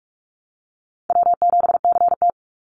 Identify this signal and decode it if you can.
Morse code, heard as a keyed tone, about 44 wpm, 700 Hz. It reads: W7CT
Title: Easy Call Signs Repeated 3x - 45wpm